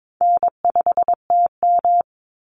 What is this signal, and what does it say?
Morse code, heard as a keyed tone, about 22 wpm, 700 Hz. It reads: N5TM
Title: Easy Call Signs Repeated 3x - 22wpm